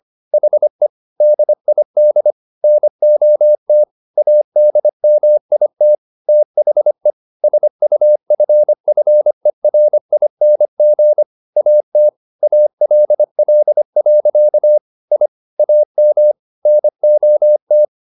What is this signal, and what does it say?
Morse code, heard as a keyed tone, about 25 wpm, 600 Hz. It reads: HE DID NOT ADMIT THE SUFFERING AT ALL. I AM NOT
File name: the old man and the sea part 1 25wpm